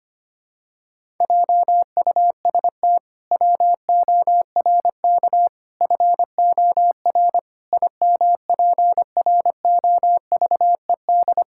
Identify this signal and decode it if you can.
Morse code, heard as a keyed tone, about 25 wpm, 700 Hz. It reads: JUST WORK FOR IMPROVED
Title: Pangram Challenge v2 - 25wpm